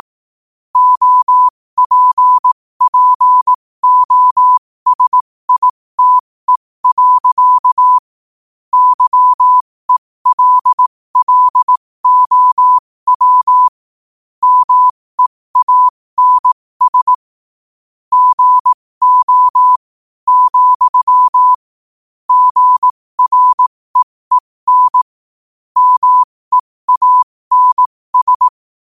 Morse code, heard as a keyed tone, about 18 wpm, spaced as for 15 wpm, 1000 Hz. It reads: OPPOSITE. YELLOW MEANS GO, GREEN MEANS
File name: qotdfiles / 15wpm